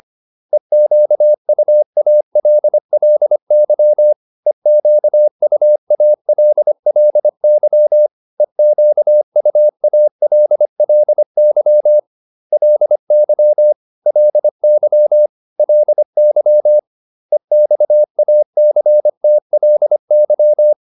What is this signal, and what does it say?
Morse code, heard as a keyed tone, about 25 wpm, 600 Hz. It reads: EQUALLY EQUALLY EQUALLY LY LY LY EXACTLY